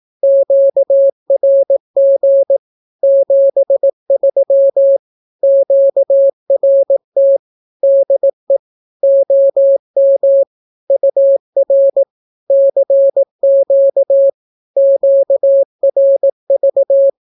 Morse code, heard as a keyed tone, about 18 wpm, 550 Hz. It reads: QRG 73 QRT DE OM UR CQ QRV
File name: naukapodstaw / skroty / skroty